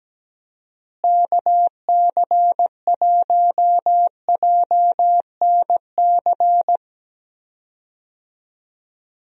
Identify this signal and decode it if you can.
Morse code, heard as a keyed tone, about 17 wpm, 700 Hz. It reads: KC1JNC
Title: USA Call Signs - 6 Characters - Repeated 2x - 17wpm